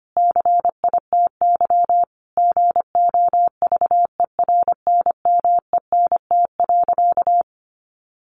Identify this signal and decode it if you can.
Morse code, heard as a keyed tone, about 25 wpm, 700 Hz. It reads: CITY GOVERNMENT.
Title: Sentences from Top 1100 Words 25wpm